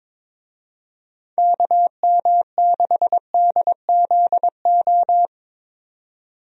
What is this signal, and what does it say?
Morse code, heard as a keyed tone, about 22 wpm, 700 Hz. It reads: KM6DZO
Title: USA Call Signs - 6 Characters - 22wpm